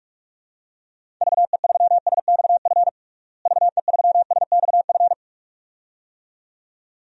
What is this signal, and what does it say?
Morse code, heard as a keyed tone, about 46 wpm, 700 Hz. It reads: VE3SXF VE3SXF
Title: DXCC-100 Call Signs - 6 Characters - Repeated 2x - 45wpm